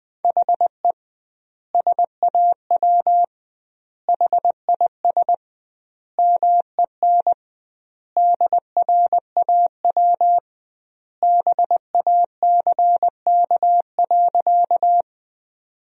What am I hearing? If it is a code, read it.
Morse code, heard as a keyed tone, about 21 wpm, 700 Hz. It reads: HE SAW HIS MEN DRAW BACK.
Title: 2x Word-Spacing - Sentences from Top 300 Words - 20wpm